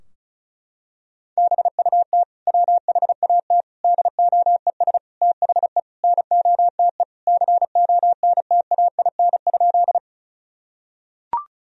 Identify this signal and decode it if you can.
Morse code, heard as a keyed tone, about 36 wpm, 700 Hz. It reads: BUT WHAT DOES THE NOTE CONTAIN?
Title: Sentences from Top 1000 Words - Review Part 2 - 35wpm